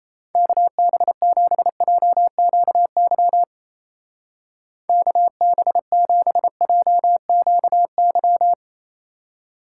Morse code, heard as a keyed tone, about 35 wpm, 700 Hz. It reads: KB7JQY KB7JQY